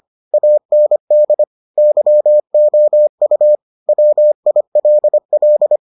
Morse code, heard as a keyed tone, about 25 wpm, 600 Hz. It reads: AND YOU WILL